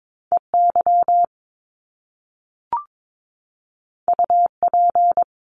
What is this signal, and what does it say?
Morse code, heard as a keyed tone, about 22 wpm, 700 Hz. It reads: EY UP
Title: Top Two Letter Combinations in Words 22wpm